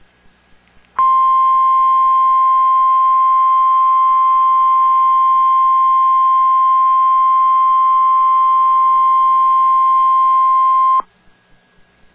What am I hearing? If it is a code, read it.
Emergency Alert System (SAME) tones (the 1050 Hz attention tone). It is heard from 1.0 to 11.1 s.